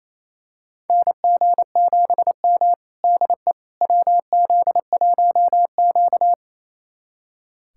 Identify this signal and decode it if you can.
Morse code, heard as a keyed tone, about 28 wpm, 700 Hz. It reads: NG7M DE WZ1Q
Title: USA Callsign DE Callsign - 4 Characters - 28wpm